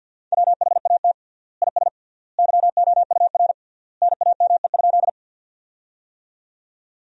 Morse code, heard as a keyed tone, about 52 wpm, 700 Hz. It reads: WHAT IS YOUR NAME?
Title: 3x Word-Spacing - Sentences from Top 200 Words - 50wpm